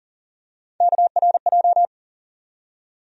Morse code, heard as a keyed tone, about 40 wpm, 700 Hz. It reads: KRJ
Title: USA Call Sign Postfixes - 3 Letters 40wpm